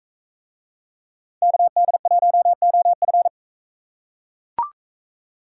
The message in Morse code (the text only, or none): KD1OF